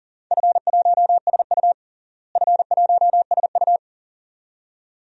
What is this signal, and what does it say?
Morse code, heard as a keyed tone, about 37 wpm, 700 Hz. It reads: F1SU F1SU
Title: DXCC-100 Call Signs - 4 Characters - Repeated 2x - 40wpm